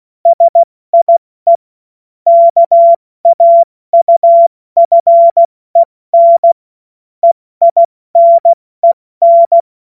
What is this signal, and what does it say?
Morse code, heard as a keyed tone, about 16 wpm, 680 Hz. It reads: SIE KAUFEN EINEN
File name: Morsekurs / Witze / Witz 28. September 2021, 70 F80